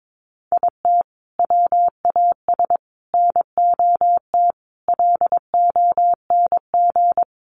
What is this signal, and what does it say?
Morse code, heard as a keyed tone, about 22 wpm, 700 Hz. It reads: IT WAS NOT LONG
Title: Sets of 5 Words Encore2 22wpm